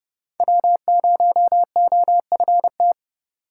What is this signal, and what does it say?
Morse code, heard as a keyed tone, about 29 wpm, 700 Hz. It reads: W0OFT